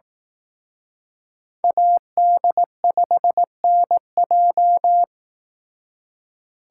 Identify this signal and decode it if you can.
Morse code, heard as a keyed tone, about 18 wpm, 700 Hz. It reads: AD5NJ
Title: Speed Racing - USA 2x2 Call Signs 15wpm